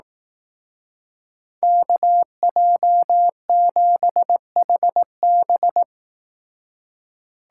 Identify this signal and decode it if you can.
Morse code, heard as a keyed tone, about 18 wpm, 700 Hz. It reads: KJ7HB
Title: Speed Racing - USA 2x2 Call Signs 15wpm